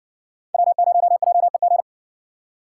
Morse code, heard as a keyed tone, about 60 wpm, 700 Hz. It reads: W0JEG